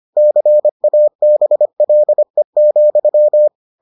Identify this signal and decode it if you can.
Morse code, heard as a keyed tone, about 25 wpm, 600 Hz. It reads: CABLE,